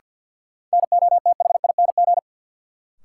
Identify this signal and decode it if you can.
Morse code, heard as a keyed tone, about 53 wpm, 700 Hz. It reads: NOTHING
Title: Sets of 4 Words - New Word Drill - 50wpm